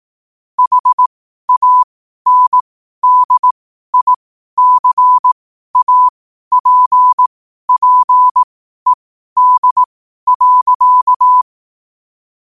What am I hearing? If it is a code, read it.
Morse code, heard as a keyed tone, about 18 wpm, spaced as for 12 wpm, 1000 Hz. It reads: HANDICAPPED.
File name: qotdfiles / 12wpm